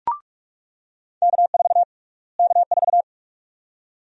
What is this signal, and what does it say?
Morse code, heard as a keyed tone, about 46 wpm, 700 Hz. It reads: K4 K4
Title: USA Call Sign Prefixes - Repeated 2x - 45wpm